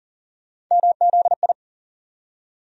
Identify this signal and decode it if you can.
Morse code, heard as a keyed tone, about 40 wpm, 700 Hz. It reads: MZI
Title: USA Call Sign Postfixes - 3 Letters 40wpm